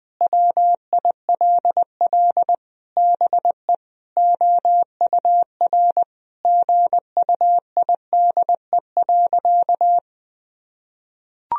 Morse code, heard as a keyed tone, about 20 wpm, 700 Hz. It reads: WILL BE OUR GUIDE.
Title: Sentences from Top 1000 Words - Review Part 3 - 20wpm